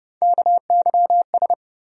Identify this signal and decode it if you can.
Morse code, heard as a keyed tone, about 30 wpm, 700 Hz. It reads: KYS
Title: USA Call Sign Postfixes - 3 Letters 30wpm